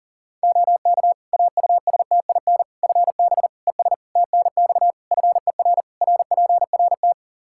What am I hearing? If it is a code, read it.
Morse code, heard as a keyed tone, about 40 wpm, 700 Hz. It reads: OK AUSTIN FB ES TNX FER RPRT
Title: Practice QSO - Exchange 2 - Info - 40wpm